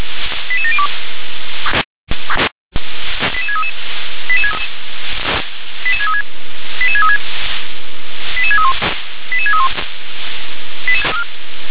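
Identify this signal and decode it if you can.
Five-tone selective call (ZVEI), heard as quick runs of five tones, tones of about 70 ms. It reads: 88702 88633 88636 88636 88631 88631
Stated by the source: ZVEI five-tone selective call 88732 received with an SDR.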